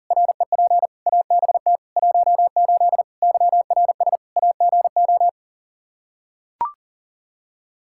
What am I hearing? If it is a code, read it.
Morse code, heard as a keyed tone, about 40 wpm, 700 Hz. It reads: REP ABT 18 YRS AGO
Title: Practice QSO - Exchange 3 - More Info - 40wpm